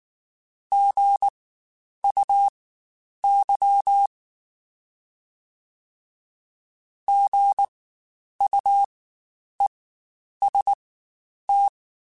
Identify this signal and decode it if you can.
Morse code, heard as a keyed tone, about 19 wpm, spaced as for 9 wpm, 800 Hz. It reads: GUY GUEST
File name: Spaced Alphas / Spaced Alphas mp3s / Gs sp @18